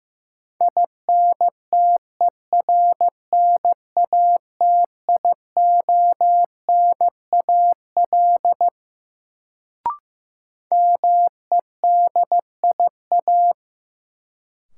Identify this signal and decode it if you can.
Morse code, heard as a keyed tone, about 15 wpm, 700 Hz. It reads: INTERNATIONAL MEDIA
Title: Sentences from Top 1000 Words - Review Part 4 - New Word Drill - 15wpm